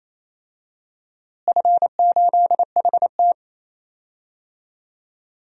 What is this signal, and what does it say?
Morse code, heard as a keyed tone, about 28 wpm, 700 Hz. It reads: F8HT
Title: DXCC-100 Call Signs - 4 Characters - Repeated 2x - 28wpm